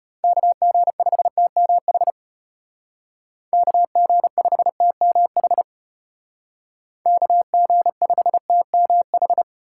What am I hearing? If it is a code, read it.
Morse code, heard as a keyed tone, about 32 wpm, 700 Hz. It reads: KG5TMH KG5TMH KG5TMH